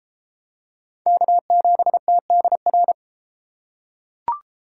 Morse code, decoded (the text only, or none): K7TDR